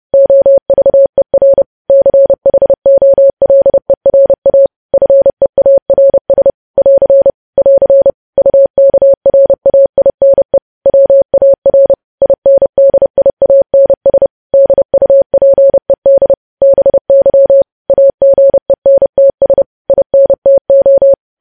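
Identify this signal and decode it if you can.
Morse code, heard as a keyed tone, about 30 wpm, 550 Hz. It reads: OVER CHOLERA FEARS <AR> <AR> UKRAINE WAR INDIANS DUPED BY AGENTS INTO
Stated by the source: News headlines sent in Morse at 30 words per minute.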